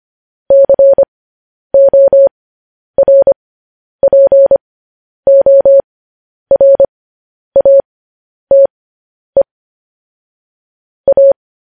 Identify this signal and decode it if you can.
Morse code, heard as a keyed tone, about 25 wpm, spaced as for 10 wpm, 550 Hz. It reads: CORPORATE A
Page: news headlines in morse code at 10 wpm